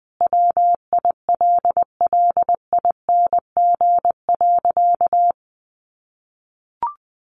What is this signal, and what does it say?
Morse code, heard as a keyed tone, about 20 wpm, 700 Hz. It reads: WILLING.
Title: Sentences from Top 1400 Words 20wpm